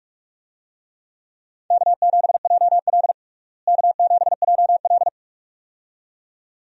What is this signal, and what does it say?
Morse code, heard as a keyed tone, about 46 wpm, 700 Hz. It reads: K7JL K7JL
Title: USA Call Signs - 4 Characters - Repeated 2x - 45wpm